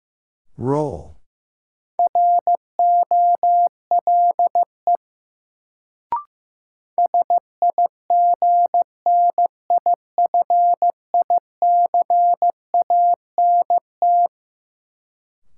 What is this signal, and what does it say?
Morse code, heard as a keyed tone, about 16 wpm, 700 Hz. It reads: ROLE SIGNIFICANT
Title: Sentences from Top 1000 Words - Review Part 6 - New Word Drill - 15wpm